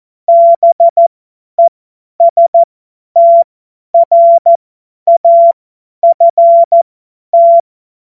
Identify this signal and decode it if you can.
Morse code, heard as a keyed tone, about 14 wpm, spaced as for 10 wpm, 680 Hz. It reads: BESTRAFT
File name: Morsekurs / Witze / Witz 28. September 2021, 50 F70